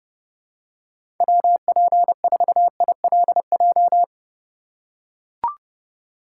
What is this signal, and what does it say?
Morse code, heard as a keyed tone, about 30 wpm, 700 Hz. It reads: WP4ILJ
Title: USA Call Signs - 6 Characters - 4x Character Spacing - 30wpm